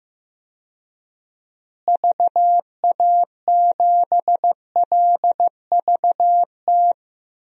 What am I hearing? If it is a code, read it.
Morse code, heard as a keyed tone, about 15 wpm, 700 Hz. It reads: VA7LVT
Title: DXCC-100 Call Signs - 6 Characters - Repeated 3x - 15wpm